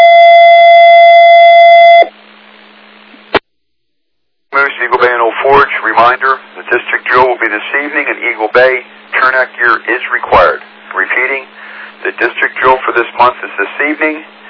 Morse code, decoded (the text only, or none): none